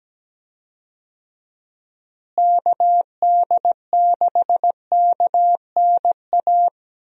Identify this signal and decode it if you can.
Morse code, heard as a keyed tone, about 17 wpm, 700 Hz. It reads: KD6KNA
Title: USA Call Signs - 6 Characters - Repeated 2x - 17wpm